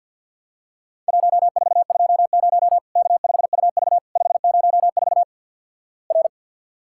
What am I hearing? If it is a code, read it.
Morse code, heard as a keyed tone, about 50 wpm, 700 Hz. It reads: 1420 K5UV 504 R
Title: Worked All Europe DX Contest - QTC Exchange - 50wpm